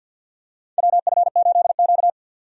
Morse code, heard as a keyed tone, about 46 wpm, 700 Hz. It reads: WF8Q
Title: USA Call Signs - 4 Characters - 50wpm